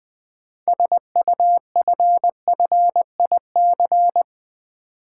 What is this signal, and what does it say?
Morse code, heard as a keyed tone, about 20 wpm, 700 Hz. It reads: SUFFIC
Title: Build A Word - 6 Letters At A Time - 12-Letter Words - 20wpm